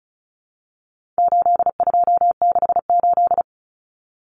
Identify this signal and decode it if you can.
Morse code, heard as a keyed tone, about 34 wpm, 700 Hz. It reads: 8268
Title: Numbers - 4-Digit 35wpm